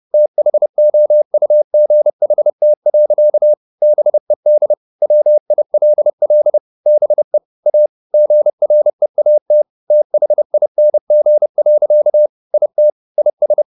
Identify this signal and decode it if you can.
Morse code, heard as a keyed tone, about 30 wpm, 600 Hz. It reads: THOUGHT. BED WILL BE A GREAT THING. IT IS